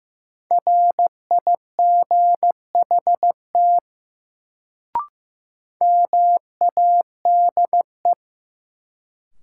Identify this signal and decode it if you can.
Morse code, heard as a keyed tone, about 15 wpm, 700 Hz. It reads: RIGHT MADE
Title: Sentences from Top 200 Words - New Word Drill - 15wpm